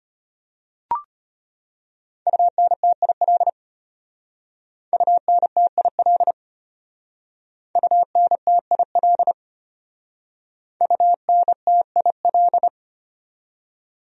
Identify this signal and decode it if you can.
Morse code, heard as a keyed tone, about 31 wpm, 700 Hz. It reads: UNTIL UNTIL UNTIL UNTIL
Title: Speed-Racing Top 300 Words 25wpm